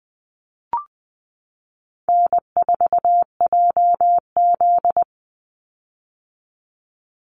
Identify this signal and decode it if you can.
Morse code, heard as a keyed tone, about 20 wpm, 700 Hz. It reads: N4JZ
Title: USA Call Signs 20wpm